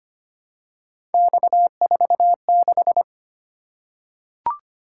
Morse code, heard as a keyed tone, about 25 wpm, 700 Hz. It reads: X46
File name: Three Letter-Number Combinations - 25wpm